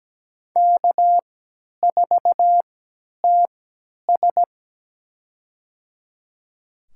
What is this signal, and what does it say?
Morse code, heard as a keyed tone, about 17 wpm, spaced as for 10 wpm, 700 Hz. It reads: K4TS
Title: USA Call Signs - 4 Characters - 3x Character Spacing - 17wpm